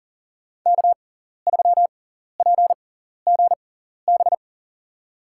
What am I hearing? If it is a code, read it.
Morse code, heard as a keyed tone, about 39 wpm, spaced as for 14 wpm, 700 Hz. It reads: K3PGB
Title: USA Call Signs - 5 Characters - 6x Character Spacing - 40wpm